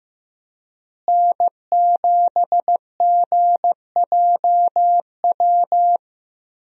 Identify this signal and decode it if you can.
Morse code, heard as a keyed tone, about 16 wpm, 700 Hz. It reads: N7GJW
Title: USA Call Signs - 5 Characters - Repeated 2x - 15wpm